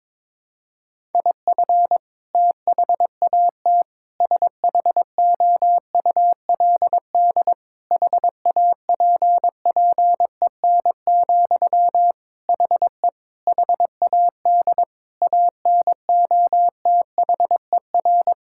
Morse code, heard as a keyed tone, about 22 wpm, 700 Hz. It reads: IF THAT SHOULD HAPPEN, HE HAD ANOTHER
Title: Sentences from Top 1000 Words - Review Part 6 - 22wpm